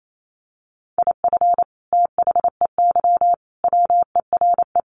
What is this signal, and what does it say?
Morse code, heard as a keyed tone, about 28 wpm, 700 Hz. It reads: IF THEY WERE
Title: Sentences from Top 1200 Words 28wpm